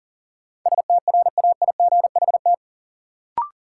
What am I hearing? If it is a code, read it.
Morse code, heard as a keyed tone, about 39 wpm, 700 Hz. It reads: STRAIGHT
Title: Sentences from Top 1000 Words - New Word Drill - 40wpm